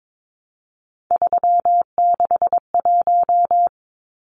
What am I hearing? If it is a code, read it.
Morse code, heard as a keyed tone, about 22 wpm, 700 Hz. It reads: 361
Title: Numbers - 3-Digit 22wpm